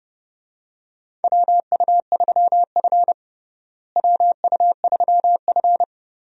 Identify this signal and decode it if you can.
Morse code, heard as a keyed tone, about 30 wpm, 700 Hz. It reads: WU3F WU3F
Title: USA Call Signs - 4 Characters - Repeated 3x - 30wpm